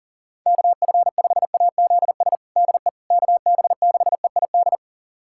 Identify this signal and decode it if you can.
Morse code, heard as a keyed tone, about 40 wpm, 700 Hz. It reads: KF5AZS DE KB6EID
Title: USA Callsign DE Callsign - 6 Characters - 40wpm